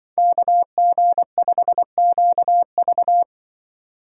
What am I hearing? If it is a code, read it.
Morse code, heard as a keyed tone, about 24 wpm, 700 Hz. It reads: KG5QV
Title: Speed Racing - USA 2x2 Call Signs 20wpm